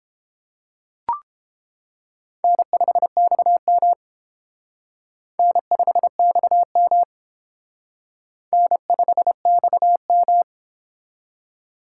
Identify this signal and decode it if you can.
Morse code, heard as a keyed tone, about 28 wpm, 700 Hz. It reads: N5XM N5XM N5XM